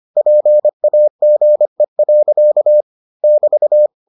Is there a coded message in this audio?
Yes: Morse code from 0.2 to 3.9 s, heard as a keyed tone, about 25 wpm, 600 Hz. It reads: PAGE. <BT>